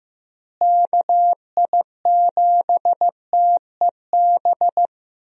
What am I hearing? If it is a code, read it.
Morse code, heard as a keyed tone, about 15 wpm, 700 Hz. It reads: KI7TEB
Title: USA Call Signs - 6 Characters - Repeated 3x - 15wpm